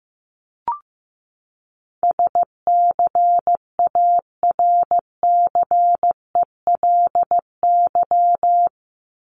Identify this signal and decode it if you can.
Morse code, heard as a keyed tone, about 15 wpm, 700 Hz. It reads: SCARCELY
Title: LY Suffix - Top 100 Words with Base Words 15wpm